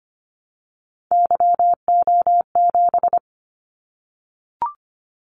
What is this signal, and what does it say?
Morse code, heard as a keyed tone, about 25 wpm, 700 Hz. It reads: YO7
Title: Callsign Prefixes for Top 25 DXCC - 25wpm